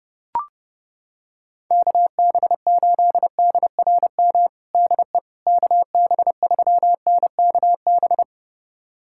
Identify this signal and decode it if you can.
Morse code, heard as a keyed tone, about 30 wpm, 700 Hz. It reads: KB8DRM DE KB3NKB
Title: USA Callsign DE Callsign - 6 Characters - 30wpm